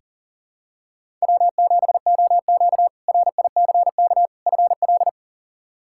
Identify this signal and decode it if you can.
Morse code, heard as a keyed tone, about 40 wpm, 700 Hz. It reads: W7OQ RICK FL
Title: K1USN SST QSO Practice - 40wpm